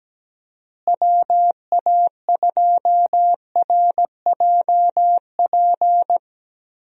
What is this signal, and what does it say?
Morse code, heard as a keyed tone, about 17 wpm, 700 Hz. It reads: WA2RJP
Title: USA Call Signs - 6 Characters - Repeated 2x - 17wpm